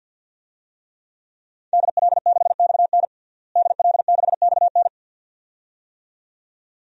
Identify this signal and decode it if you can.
Morse code, heard as a keyed tone, about 50 wpm, 700 Hz. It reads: DL6XN DL6XN
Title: DXCC-100 Call Signs - 5 Characters - Repeated 2x - 50wpm